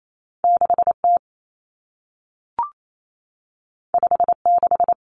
Morse code, heard as a keyed tone, about 28 wpm, 700 Hz. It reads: 6T 56
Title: Common Cut Numbers - 2-Digit 28wpm